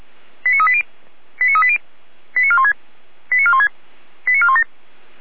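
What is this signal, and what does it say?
Five-tone selective call (ZVEI), heard as quick runs of five tones, tones of about 70 ms. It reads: 89380 89380 89416 89416 89417
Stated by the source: ZVEI five-tone selective call received via SDR.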